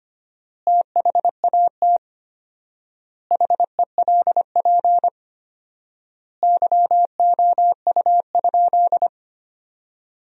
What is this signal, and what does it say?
Morse code, heard as a keyed tone, about 25 wpm, 700 Hz. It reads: THAT HELP YOU?
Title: 4x Word-Spacing - Sentences from Top 200 Words - 25wpm